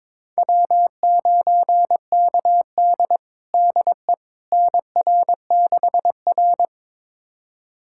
Morse code, heard as a keyed tone, about 22 wpm, 700 Hz. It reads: W9KD DE NR6R
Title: USA Callsign DE Callsign - 4 Characters - 22wpm